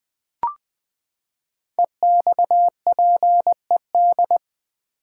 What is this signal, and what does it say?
Morse code, heard as a keyed tone, about 20 wpm, 700 Hz. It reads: EXPED
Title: Build A Word - 5 Letters At A Time - 10-Letter Words - 20wpm